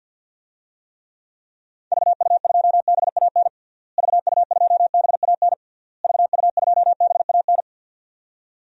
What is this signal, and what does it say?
Morse code, heard as a keyed tone, about 50 wpm, 700 Hz. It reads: VU2BAN VU2BAN VU2BAN
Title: DXCC-100 Call Signs - 6 Characters - Repeated 3x - 50wpm